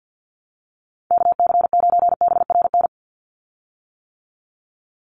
Morse code, heard as a keyed tone, about 50 wpm, 700 Hz. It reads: KC9BRN
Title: USA Call Signs 50wpm